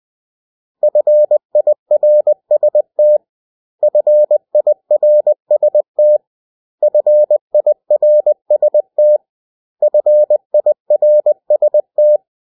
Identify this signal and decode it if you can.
Morse code, heard as a keyed tone, about 20 wpm, 600 Hz. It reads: FIRST FIRST FIRST FIRST
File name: cw / 100 Common English Words